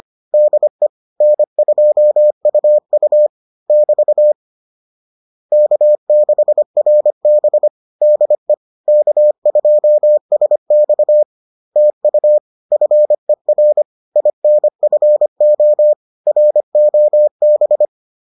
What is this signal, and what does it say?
Morse code, heard as a keyed tone, about 25 wpm, 600 Hz. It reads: DE N2UU <BT> K6RB DE K2SX TU FER INFO ROB